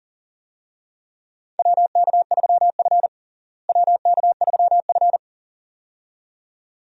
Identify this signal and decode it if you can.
Morse code, heard as a keyed tone, about 37 wpm, 700 Hz. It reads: WK3F WK3F
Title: USA Call Signs - 4 Characters - Repeated 2x - 40wpm